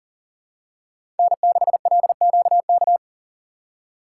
Morse code, heard as a keyed tone, about 40 wpm, 700 Hz. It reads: N6LQK
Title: USA Call Signs - 5 Characters - 2x Character Spacing - 40wpm